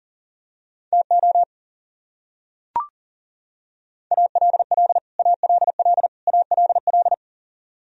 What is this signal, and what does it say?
Morse code, heard as a keyed tone, about 40 wpm, 700 Hz. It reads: TO ALL ALL ALL
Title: Top 100 Words Repeated 40wpm